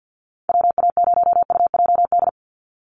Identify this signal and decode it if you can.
Morse code, heard as a keyed tone, about 50 wpm, 700 Hz. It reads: WA0UJD
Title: USA Call Signs 50wpm